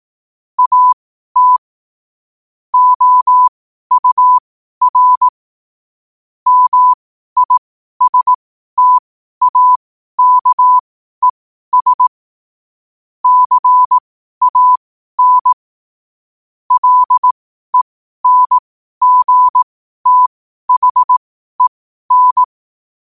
Morse code, heard as a keyed tone, about 18 wpm, spaced as for 12 wpm, 1000 Hz. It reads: AT OUR MISTAKES CAN LENGTHEN
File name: qotdfiles / 12wpm